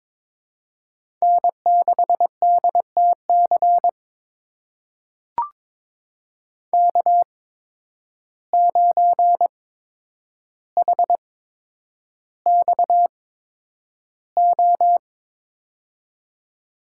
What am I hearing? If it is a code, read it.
Morse code, heard as a keyed tone, about 22 wpm, spaced as for 9 wpm, 700 Hz. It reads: N6DTC K9HXO
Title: USA Call Signs - 5 Characters - 8x Character Spacing - 22wpm